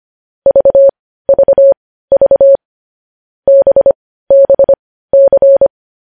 Morse code, heard as a keyed tone, about 25 wpm, spaced as for 15 wpm, 550 Hz. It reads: VVV BBC
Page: News Headlines in Morse Code at 15 WPM